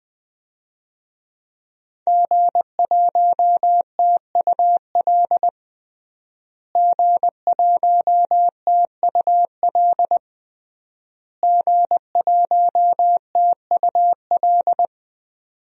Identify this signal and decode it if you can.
Morse code, heard as a keyed tone, about 20 wpm, 700 Hz. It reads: G1TUL G1TUL G1TUL
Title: DXCC-100 Call Signs - 5 Characters - Repeated 3x - 20wpm